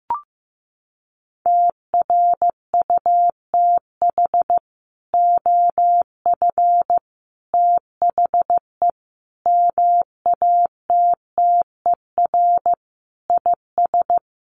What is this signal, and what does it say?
Morse code, heard as a keyed tone, about 15 wpm, 700 Hz. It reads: TRUTH OF THE MATTER IS
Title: Sets of 5 Words Encore2 15wpm